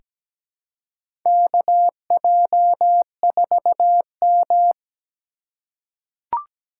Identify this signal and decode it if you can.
Morse code, heard as a keyed tone, about 17 wpm, 700 Hz. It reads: KJ4M
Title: USA Call Signs - 4 Characters - 8x Character Spacing - 17wpm